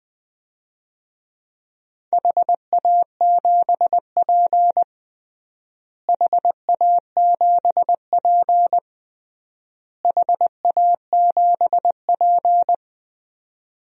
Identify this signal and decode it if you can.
Morse code, heard as a keyed tone, about 20 wpm, 700 Hz. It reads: HA7P HA7P HA7P
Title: DXCC-100 Call Signs - 4 Characters - Repeated 3x - 20wpm